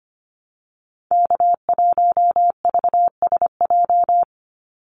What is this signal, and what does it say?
Morse code, heard as a keyed tone, about 25 wpm, 700 Hz. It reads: K1VSJ
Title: Call Signs - 25wpm